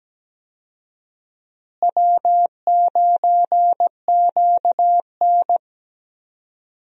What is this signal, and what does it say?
Morse code, heard as a keyed tone, about 17 wpm, 700 Hz. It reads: W9QN
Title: USA Call Signs - 4 Characters - Repeated 2x - 17wpm